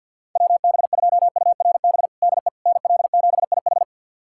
Morse code, heard as a keyed tone, about 50 wpm, 700 Hz. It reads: WB2URB DE NL7IH
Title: USA Callsign DE Callsign 50wpm